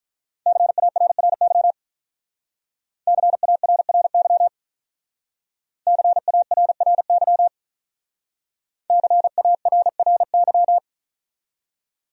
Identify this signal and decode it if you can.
Morse code, heard as a keyed tone, about 46 wpm, 700 Hz. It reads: CARRY CARRY CARRY CARRY